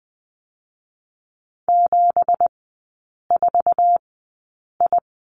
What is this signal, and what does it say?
Morse code, heard as a keyed tone, about 20 wpm, spaced as for 8 wpm, 700 Hz. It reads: 74I
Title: 212 Lesson 34e - Number 8 - Three Characters 20wpm